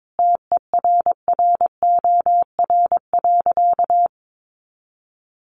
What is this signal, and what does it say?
Morse code, heard as a keyed tone, about 22 wpm, 700 Hz. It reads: TERROR.
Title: A Sound Of Thunder 22wpm